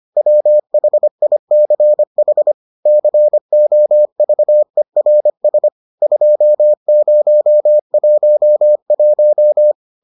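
Morse code, heard as a keyed tone, about 25 wpm, 600 Hz. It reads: WHICH COVERS 2011
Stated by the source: Morse edition of the Foundations of Amateur Radio episodes.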